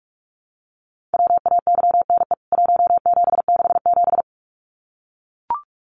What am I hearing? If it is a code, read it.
Morse code, heard as a keyed tone, about 44 wpm, 700 Hz. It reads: WAYNE 1767
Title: CWT Exchange 45wpm